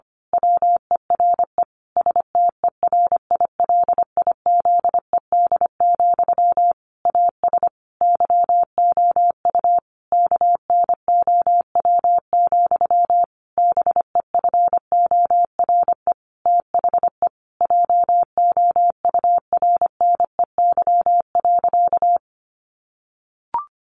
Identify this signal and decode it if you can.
Morse code, heard as a keyed tone, about 25 wpm, 700 Hz. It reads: WERE STERILIZED, AS YOU KNOW, BEFORE THE JOURNEY.
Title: A Sound Of Thunder 25wpm